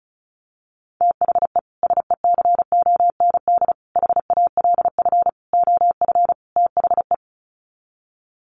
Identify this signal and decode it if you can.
Morse code, heard as a keyed tone, about 34 wpm, 700 Hz. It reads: THE SECOND HALF OF THE
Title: Sets of 5 Words 35wpm